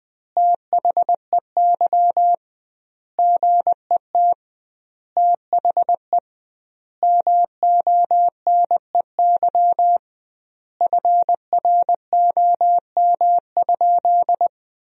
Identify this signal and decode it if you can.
Morse code, heard as a keyed tone, about 20 wpm, 700 Hz. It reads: THEY GET THE MONEY FROM?
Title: 2x Word-Spacing - Sentences from Top 400 Words - 20wpm